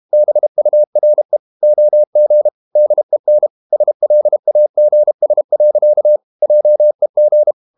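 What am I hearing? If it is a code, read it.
Morse code, heard as a keyed tone, about 32 wpm, 600 Hz. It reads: BURE OG DEN SLAGS. JEG
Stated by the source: High-speed QRQ CW at 32 WPM.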